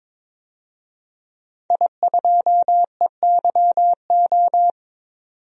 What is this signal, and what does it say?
Morse code, heard as a keyed tone, about 21 wpm, 700 Hz. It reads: I2EYO
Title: DXCC-100 Call Signs - 5 Characters - 22wpm